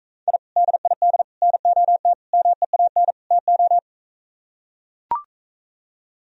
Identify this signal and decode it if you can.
Morse code, heard as a keyed tone, about 41 wpm, 700 Hz. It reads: I DID NOT MEAN TO